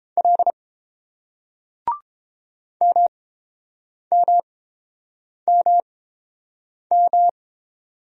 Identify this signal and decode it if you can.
Morse code, heard as a keyed tone, about 30 wpm, spaced as for 7 wpm, 700 Hz. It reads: L MMMM